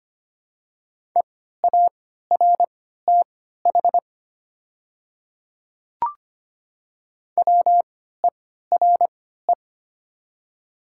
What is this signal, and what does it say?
Morse code, heard as a keyed tone, about 25 wpm, spaced as for 14 wpm, 700 Hz. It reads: EARTH WERE
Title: 3x Character Spacing - Top 200 Words - 25wpm